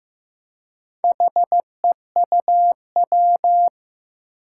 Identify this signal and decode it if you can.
Morse code, heard as a keyed tone, about 15 wpm, 700 Hz. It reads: HEUW
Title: Four Challenging Letter-Number Combinations 15wpm